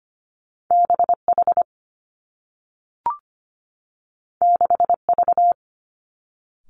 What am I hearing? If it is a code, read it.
Morse code, heard as a keyed tone, about 25 wpm, 700 Hz. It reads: BH 6V
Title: Two Challenging Letter-Number Combinations 25wpm